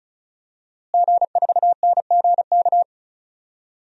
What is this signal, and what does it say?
Morse code, heard as a keyed tone, about 36 wpm, 700 Hz. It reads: G4NGK
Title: DXCC-100 Call Signs - 5 Characters - Repeated 3x - 35wpm